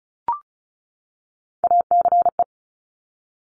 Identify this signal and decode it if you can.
Morse code, heard as a keyed tone, about 36 wpm, 700 Hz. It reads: ACE
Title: Top Three Letter Combinations in Words 35wpm